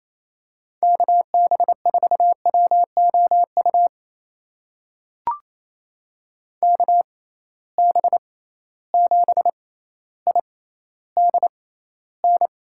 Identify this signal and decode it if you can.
Morse code, heard as a keyed tone, about 28 wpm, spaced as for 14 wpm, 700 Hz. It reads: KB4WOU KB7IDN
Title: USA Call Signs - 6 Characters - 6x Character Spacing - 28wpm